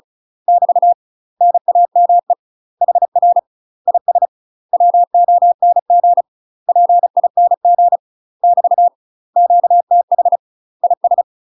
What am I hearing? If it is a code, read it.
Morse code, heard as a keyed tone, about 36 wpm, 700 Hz. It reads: <BT> NAME HR IS WONG PING <BT> QTH IS
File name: UK MASTER QSOs TRAINING / 35WPM / UK MASTER QSOs 0029